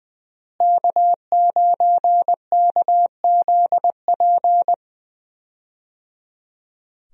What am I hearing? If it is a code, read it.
Morse code, heard as a keyed tone, about 21 wpm, 700 Hz. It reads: K9KZP
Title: USA Call Signs - 5 Characters - Repeated 3x - 20wpm